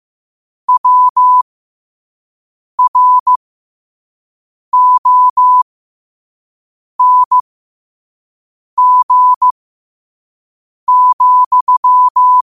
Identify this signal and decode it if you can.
Morse code, heard as a keyed tone, about 15 wpm, spaced as for 5 wpm, 1000 Hz. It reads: WRONG,
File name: qotdfiles / 5wpm